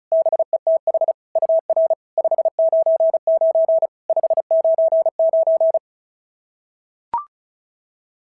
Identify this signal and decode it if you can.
Morse code, heard as a keyed tone, about 36 wpm, 650 Hz. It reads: BETH UR 599 599
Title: SOTA QSO Practice 35wpm